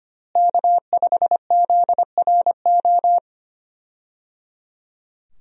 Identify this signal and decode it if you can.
Morse code, heard as a keyed tone, about 25 wpm, 700 Hz. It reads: K5ZRO
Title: USA Call Signs - 5 Characters - Repeated 3x - 25wpm